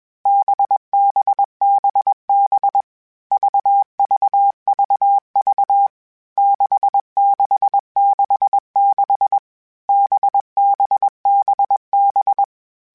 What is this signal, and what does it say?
Morse code, heard as a keyed tone, about 21 wpm, 800 Hz. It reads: BBBB VVVV 6666 BBBB
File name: Basic Sets / Basic Letter Groups Collective7 / VB46@20